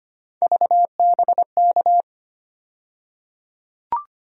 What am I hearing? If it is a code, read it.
Morse code, heard as a keyed tone, about 25 wpm, 700 Hz. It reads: VBK
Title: USA Call Sign Postfixes - 3 Letters 25wpm